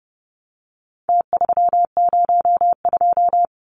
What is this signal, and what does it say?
Morse code, heard as a keyed tone, about 30 wpm, 700 Hz. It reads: T302
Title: Numbers - Contest Style - 4-Digit 30wpm